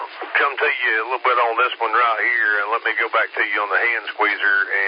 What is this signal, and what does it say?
Emergency Alert System (SAME) tones, heard as harsh buzzing bursts, with speech. It is heard from 0.6 to 4.1 s.